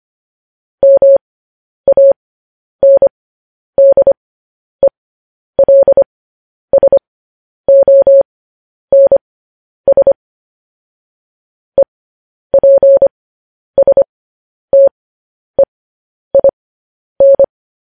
Morse code, heard as a keyed tone, about 25 wpm, spaced as for 10 wpm, 550 Hz. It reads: MANDELSONS EPSTEIN